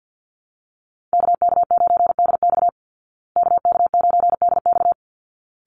Morse code, heard as a keyed tone, about 50 wpm, 700 Hz. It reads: KK9DX KK9DX
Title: Call Signs Repeated 3x - 50wpm